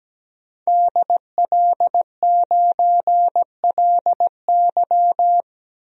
Morse code, heard as a keyed tone, about 17 wpm, 700 Hz. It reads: DL9LY
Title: DXCC-100 Call Signs - 5 Characters - Repeated 2x - 17wpm